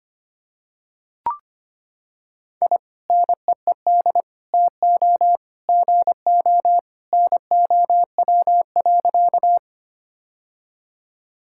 Morse code, heard as a keyed tone, about 25 wpm, 700 Hz. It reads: I NEED TO GO NOW.
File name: practice / 25wpm / Sentences with 2-Word Phrases - 25wpm